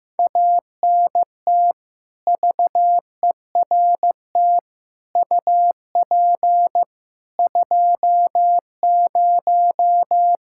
Morse code, heard as a keyed tone, about 15 wpm, 700 Hz. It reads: ANT VERT UP 20
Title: Practice QSO - Exchange 2 - Info - 15wpm